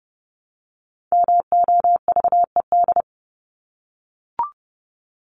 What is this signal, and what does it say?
Morse code, heard as a keyed tone, about 30 wpm, 700 Hz. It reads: MOVED
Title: ED Suffix - Top 100 Words with Base Words 30wpm